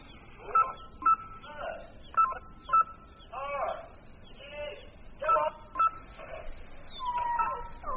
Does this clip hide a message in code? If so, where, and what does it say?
No, only speech.